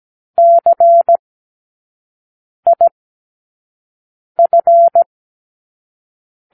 Morse code, heard as a keyed tone, about 17 wpm, 680 Hz. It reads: C I F
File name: k7qo / Lesson15(A-J)